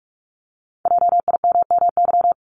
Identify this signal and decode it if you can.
Morse code, heard as a keyed tone, about 46 wpm, 700 Hz. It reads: JIMMY
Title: Multi-letter Phonemes Repeated 3x 45wpm